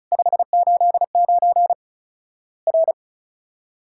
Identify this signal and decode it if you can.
Morse code, heard as a keyed tone, about 34 wpm, 700 Hz. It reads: 589 R
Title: Worked All Europe DX Contest - QTC Exchange - 35wpm